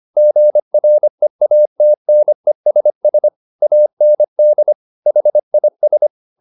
Morse code, heard as a keyed tone, about 25 wpm, 600 Hz. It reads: GREATNESS AND HIS
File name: the old man and the sea part 1 25wpm